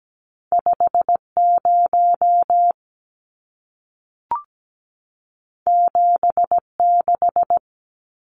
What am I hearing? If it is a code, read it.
Morse code, heard as a keyed tone, about 17 wpm, 700 Hz. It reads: 50 76
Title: Numbers - 2-Digit 17wpm